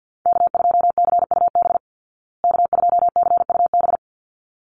Morse code, heard as a keyed tone, about 49 wpm, 700 Hz. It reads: K2CUB K2CUB
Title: Call Signs Repeated 3x - 50wpm